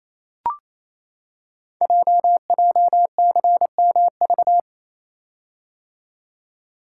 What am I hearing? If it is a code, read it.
Morse code, heard as a keyed tone, about 29 wpm, 700 Hz. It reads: JJCMV
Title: Code Groups - 5 Letters - 28wpm